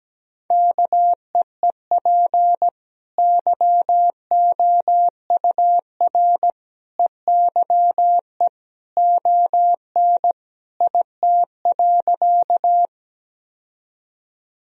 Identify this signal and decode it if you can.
Morse code, heard as a keyed tone, about 17 wpm, 700 Hz. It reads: KEEP YOUR EYE ON IT.
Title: Sentences from Top 1000 Words - Review Part 1 - 17wpm